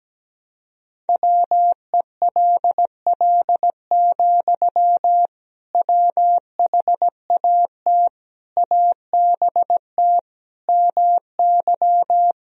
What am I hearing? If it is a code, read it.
Morse code, heard as a keyed tone, about 17 wpm, 700 Hz. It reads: WELL, WHAT ABT MY
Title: Sentences from Top 500 Words and Abbreviations 17wpm